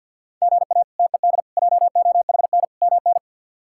Morse code, heard as a keyed tone, about 51 wpm, 700 Hz. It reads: GA TED JOHN MN
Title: K1USN SST QSO Practice - 50wpm